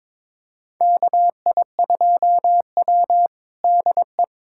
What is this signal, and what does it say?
Morse code, heard as a keyed tone, about 22 wpm, 700 Hz. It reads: KI2W DE
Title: USA Callsign DE Callsign - 4 Characters - 22wpm